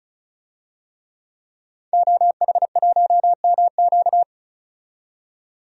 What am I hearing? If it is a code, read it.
Morse code, heard as a keyed tone, about 36 wpm, 700 Hz. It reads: OH1MQ
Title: DXCC-100 Call Signs - 5 Characters - 35wpm